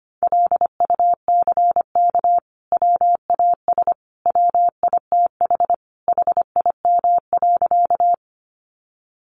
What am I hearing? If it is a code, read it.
Morse code, heard as a keyed tone, about 25 wpm, 700 Hz. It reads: LUCK WAS WITH HIM.
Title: Sentences from Top 1500 Words 25wpm